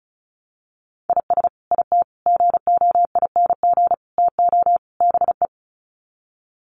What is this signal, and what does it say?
Morse code, heard as a keyed tone, about 34 wpm, 700 Hz. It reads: IS IT GOING TO BE
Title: Sets of 5 Words Encore2 35wpm